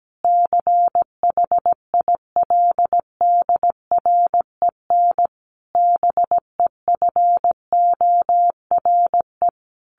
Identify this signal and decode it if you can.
Morse code, heard as a keyed tone, about 17 wpm, 700 Hz. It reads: CHILDREN BEFORE
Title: Sentences from Top 1100 Words 17wpm